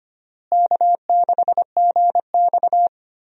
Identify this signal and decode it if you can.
Morse code, heard as a keyed tone, about 25 wpm, 700 Hz. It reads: K6GX
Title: USA Call Signs - 4 Characters - Repeated 2x - 25wpm